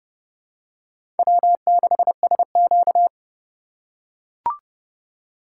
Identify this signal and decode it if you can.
Morse code, heard as a keyed tone, about 30 wpm, 700 Hz. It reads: W6SQ
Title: USA Call Signs - 4 Characters - 4x Character Spacing - 30wpm